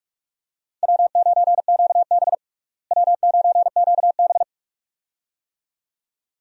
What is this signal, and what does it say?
Morse code, heard as a keyed tone, about 44 wpm, 700 Hz. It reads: W9QB W9QB
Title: USA Call Signs - 4 Characters - Repeated 2x - 45wpm